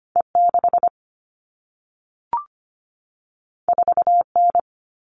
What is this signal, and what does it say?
Morse code, heard as a keyed tone, about 25 wpm, 700 Hz. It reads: E6 4N
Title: All Cut Numbers - 2-Digit 25wpm